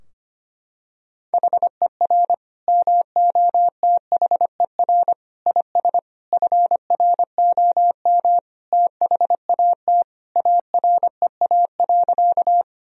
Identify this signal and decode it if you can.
Morse code, heard as a keyed tone, about 25 wpm, 700 Hz. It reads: HER MOTHER IS FROM THAT AREA.
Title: Sentences from Top 1000 Words - Review Part 2 - 25wpm